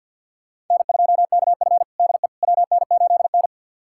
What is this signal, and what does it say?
Morse code, heard as a keyed tone, about 52 wpm, 700 Hz. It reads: N2KF DE WN8N
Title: USA Callsign DE Callsign - 4 Characters - 50wpm